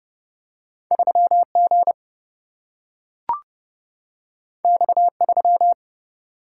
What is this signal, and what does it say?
Morse code, heard as a keyed tone, about 31 wpm, 700 Hz. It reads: 3 G X 3
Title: Single-Letter Single-Number 30wpm